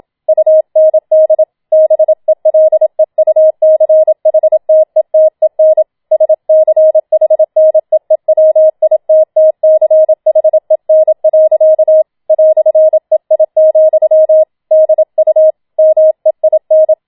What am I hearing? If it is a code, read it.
Morse code, heard as a keyed tone, about 27 wpm, 630 Hz. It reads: UND BELEUCHTETEN SCHNEEWITTCHEN. "EI, DU MEIN